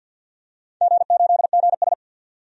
Morse code, heard as a keyed tone, about 52 wpm, 700 Hz. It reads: G8GS
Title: DXCC-100 Call Signs - 4 Characters - 50wpm